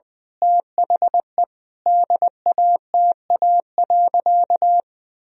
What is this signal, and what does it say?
Morse code, heard as a keyed tone, about 21 wpm, 700 Hz. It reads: THE DATA.
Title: Sentences from Top 1000 Words - Review Part 3 - 20wpm